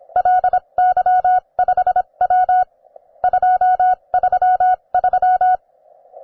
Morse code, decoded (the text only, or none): LY5W 233